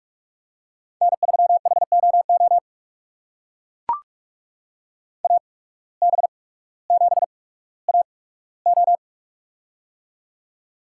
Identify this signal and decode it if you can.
Morse code, heard as a keyed tone, about 44 wpm, spaced as for 19 wpm, 700 Hz. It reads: N3HOO AB7AO
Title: USA Call Signs - 5 Characters - 8x Character Spacing - 45wpm